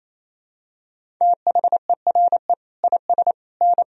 Morse code, heard as a keyed tone, about 28 wpm, 700 Hz. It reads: THERE IS N
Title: Sentences from Top 500 Words and Abbreviations 28wpm